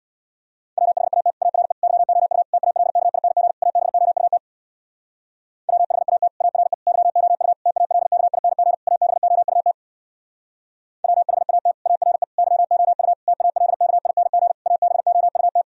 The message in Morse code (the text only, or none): WHAT ARE YOU TALKING ABOUT WHAT ARE YOU TALKING ABOUT WHAT ARE YOU TALKING ABOUT